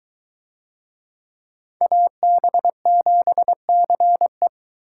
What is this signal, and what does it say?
Morse code, heard as a keyed tone, about 22 wpm, 700 Hz. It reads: AB7CE